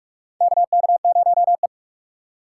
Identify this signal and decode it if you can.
Morse code, heard as a keyed tone, about 42 wpm, 700 Hz. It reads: KK0E
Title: USA Call Signs - 6x Character Spacing - 45wpm